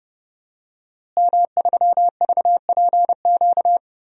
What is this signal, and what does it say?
Morse code, heard as a keyed tone, about 30 wpm, 700 Hz. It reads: M3VPQ
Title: DXCC-100 Call Signs - 5 Characters - Repeated 3x - 30wpm